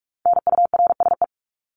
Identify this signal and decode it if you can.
Morse code, heard as a keyed tone, about 44 wpm, 700 Hz. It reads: NURSE
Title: Top 1751-2000 Words 45wpm